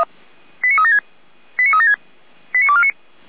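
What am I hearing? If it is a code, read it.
Five-tone selective call (ZVEI), heard as quick runs of five tones, tones of about 70 ms. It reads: 89376 89376 89249
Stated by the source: ZVEI five-tone selective call received on an SDR, fire brigade Feuerwehr Orlamünde.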